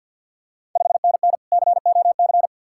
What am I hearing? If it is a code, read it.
Morse code, heard as a keyed tone, about 52 wpm, 700 Hz. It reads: 5NN COC
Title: Ontario QSO Party Exchange 50wpm